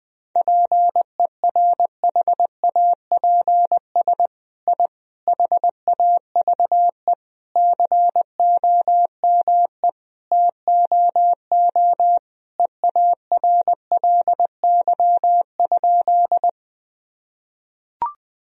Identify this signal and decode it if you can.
Morse code, heard as a keyed tone, about 20 wpm, 700 Hz. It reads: PERHAPS I HAVE COME TOO EARLY?
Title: Sentences from Top 1000 Words - Review Part 2 - 20wpm